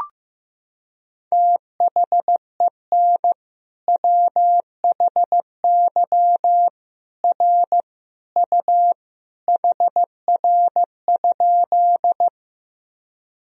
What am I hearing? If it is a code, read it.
Morse code, heard as a keyed tone, about 15 wpm, 700 Hz. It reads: THEN WHY R U HR?
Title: Sentences from Top 200 Words and Abbreviations 15wpm